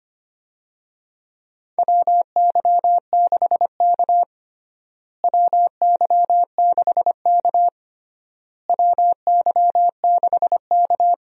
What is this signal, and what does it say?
Morse code, heard as a keyed tone, about 25 wpm, 700 Hz. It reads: WY6K WY6K WY6K
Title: USA Call Signs - 4 Characters - Repeated 3x - 25wpm